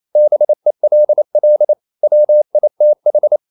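Morse code, heard as a keyed tone, about 28 wpm, 600 Hz. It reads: BELL WITH
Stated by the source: Morse code practice at 28 wpm, text of Ken Follett's novel Never in English, from CW med Gnister.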